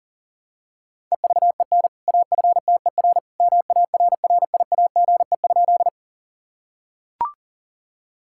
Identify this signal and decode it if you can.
Morse code, heard as a keyed tone, about 40 wpm, 700 Hz. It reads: EVEN AFTER MARRIAGE?
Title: Sentences from Top 1000 Words - Review Part 5 - 40wpm